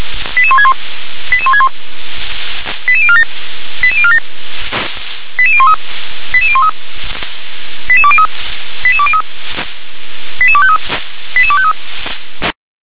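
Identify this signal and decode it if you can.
Five-tone selective call (ZVEI), heard as quick runs of five tones, tones of about 70 ms. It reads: 88161 88161 88047 88047 88013 88013 88203 88203 88253 88253
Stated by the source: Five-tone ZVEI selective call received on an SDR.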